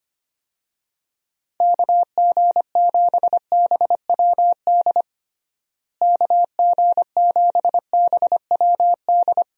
Morse code, heard as a keyed tone, about 25 wpm, 700 Hz. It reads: KG7BWD KG7BWD
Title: USA Call Signs - 6 Characters - Repeated 3x - 25wpm